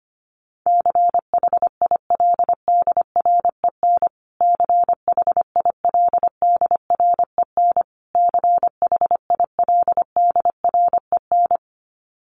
Morse code, heard as a keyed tone, about 25 wpm, 700 Hz. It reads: CHILDREN CHILDREN CHILDREN
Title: Top 500 Words Repeated 25wpm